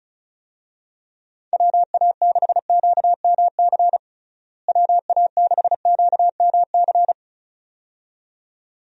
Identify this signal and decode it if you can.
Morse code, heard as a keyed tone, about 36 wpm, 700 Hz. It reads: WA6QMC WA6QMC
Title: USA Call Signs - 6 Characters - Repeated 2x - 35wpm